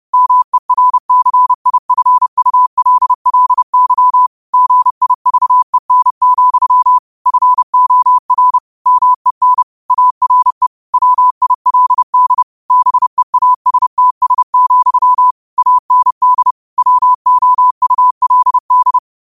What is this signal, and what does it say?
Morse code, heard as a keyed tone, about 30 wpm, 1000 Hz. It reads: MERCIFULLY GIVEN, FOR MEN ARE WILD BEASTS, AND WOULD